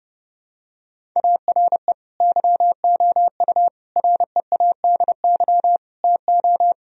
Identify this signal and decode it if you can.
Morse code, heard as a keyed tone, about 30 wpm, 700 Hz. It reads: ARE YOU READY TO
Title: Sentences from Top 1000 Words - Review Part 1 - 30wpm